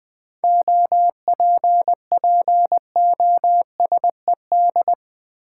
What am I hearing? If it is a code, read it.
Morse code, heard as a keyed tone, about 20 wpm, 700 Hz. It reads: OPPOSED
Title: 7 Letter Words - First 3 Letters Given - 20wpm